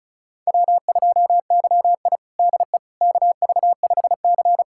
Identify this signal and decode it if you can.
Morse code, heard as a keyed tone, about 36 wpm, 700 Hz. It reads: W2YI DE KV5C
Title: USA Callsign DE Callsign - 4 Characters - 35wpm